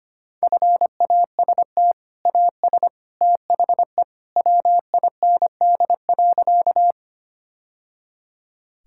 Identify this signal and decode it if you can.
Morse code, heard as a keyed tone, about 25 wpm, 700 Hz. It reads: FAST AS THE WIND.
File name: practice / 25wpm / Sentences with 2-Word Phrases - 25wpm